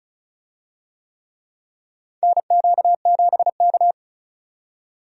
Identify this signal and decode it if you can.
Morse code, heard as a keyed tone, about 36 wpm, 700 Hz. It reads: NQ7K
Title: USA Call Signs - 4 Characters - 35wpm